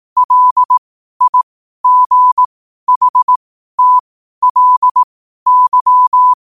Morse code, heard as a keyed tone, about 18 wpm, spaced as for 12 wpm, 1000 Hz. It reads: LIGHTLY
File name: qotdfiles / 12wpm